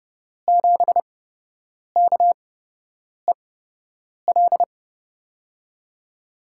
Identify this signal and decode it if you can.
Morse code, heard as a keyed tone, about 30 wpm, spaced as for 8 wpm, 700 Hz. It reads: 7KEL
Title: USA Call Signs - 6 Characters - 8x Character Spacing - 30wpm